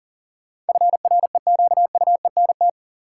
Morse code, heard as a keyed tone, about 40 wpm, 700 Hz. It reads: FREQUENT
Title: Sentences from Top 1400 Words - New Word Drill - 40wpm